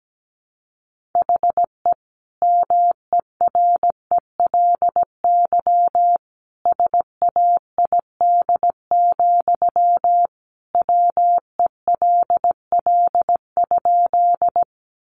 Morse code, heard as a keyed tone, about 17 wpm, 700 Hz. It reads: HE MERELY SAID, WELL?
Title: Sentences from Top 1400 Words 17wpm